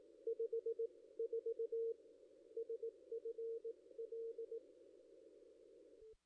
Morse code, heard as a keyed tone, about 18 wpm, spaced as for 15 wpm, 470 Hz. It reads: 54 SFL
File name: ss-cw-k5tr-op-n5rz